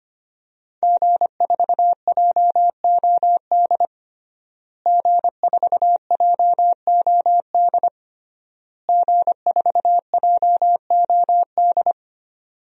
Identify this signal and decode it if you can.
Morse code, heard as a keyed tone, about 25 wpm, 700 Hz. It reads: G4JOD G4JOD G4JOD
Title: DXCC-100 Call Signs - 5 Characters - Repeated 3x - 25wpm